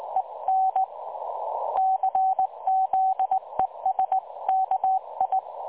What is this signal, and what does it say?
Morse code, heard as a keyed tone, about 20 wpm, 760 Hz. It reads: EN CZESKI
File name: dlazaawansowanych / nagraniazpasma / websdr recording 3565.0kHz